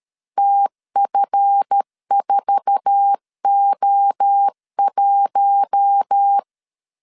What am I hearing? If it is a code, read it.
Morse code, heard as a keyed tone, about 13 wpm, 800 Hz. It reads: TF4O1
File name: LEZIONI / 8 YJ45 / 8 YJ4 12wpm 2151